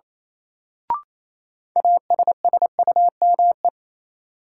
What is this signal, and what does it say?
Morse code, heard as a keyed tone, about 27 wpm, 700 Hz. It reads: ASSUME
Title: Sentences from Top 1500 Words - New Word Drill Alt - 28wpm